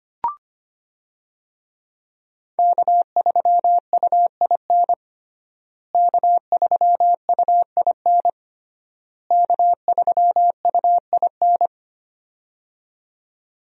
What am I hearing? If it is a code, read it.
Morse code, heard as a keyed tone, about 25 wpm, 700 Hz. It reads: K3UIN K3UIN K3UIN
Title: USA Call Signs - 5 Characters - Repeated 3x - 25wpm